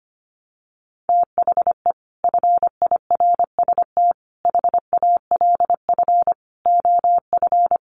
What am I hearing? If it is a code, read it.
Morse code, heard as a keyed tone, about 25 wpm, 700 Hz. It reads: THE FIRST HALF OF
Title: Sets of 5 Words 25wpm